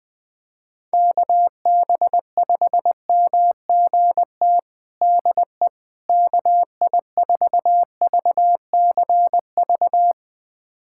Morse code, heard as a keyed tone, about 20 wpm, 700 Hz. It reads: KB5MGT DE KI4VCV
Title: USA Callsign DE Callsign - 6 Characters - 20wpm